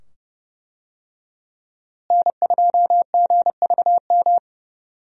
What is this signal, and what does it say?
Morse code, heard as a keyed tone, about 31 wpm, 700 Hz. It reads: N2GVM
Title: USA Call Signs - 5 Characters - Repeated 2x - 30wpm